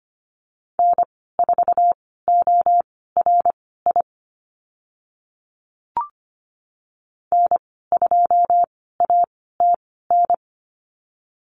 Morse code, heard as a keyed tone, about 25 wpm, spaced as for 16 wpm, 690 Hz. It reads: N4ORI N2ATN
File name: Lesson 15g - Number 5 - Call Signs 25wpm